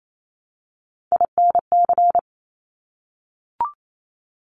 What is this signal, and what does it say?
Morse code, heard as a keyed tone, about 28 wpm, 700 Hz. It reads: INC
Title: Top Three Letter Combinations in Words 28wpm